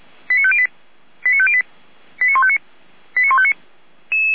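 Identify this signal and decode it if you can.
Five-tone selective call (ZVEI), heard as quick runs of five tones, tones of about 70 ms. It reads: 89508 89508 89159 89159
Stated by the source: ZVEI five-tone call received with an SDR.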